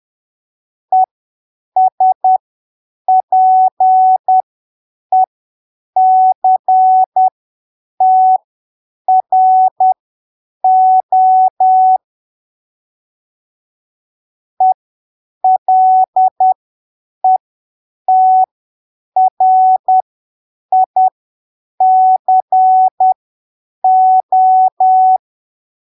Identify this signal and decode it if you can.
Morse code, heard as a keyed tone, about 10 wpm, spaced as for 7 wpm, 750 Hz. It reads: ESPECTRO ELETRICO
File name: arquivos / texto / texto 12 10ppm